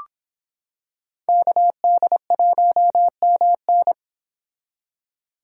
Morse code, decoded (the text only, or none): KD1MN